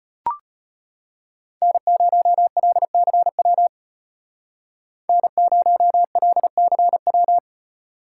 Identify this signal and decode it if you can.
Morse code, heard as a keyed tone, about 30 wpm, 700 Hz. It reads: N0LCW N0LCW
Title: Speed Racing - USA Call Signs 25wpm